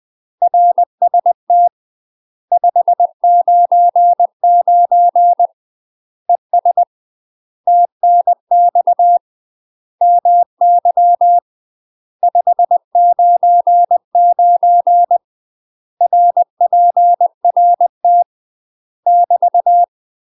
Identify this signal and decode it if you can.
Morse code, heard as a keyed tone, about 20 wpm, 700 Hz. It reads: RST 599 ES TNX MY 599 RPRT <BT>
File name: UK MASTER QSOs TRAINING / 20WPM / UK MASTER QSOs 0026